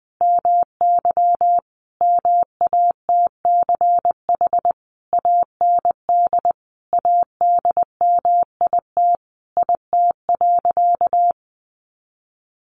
Morse code, heard as a keyed tone, about 20 wpm, 700 Hz. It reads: MY MATCH AND ADMIT IT.
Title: A Sound Of Thunder 20wpm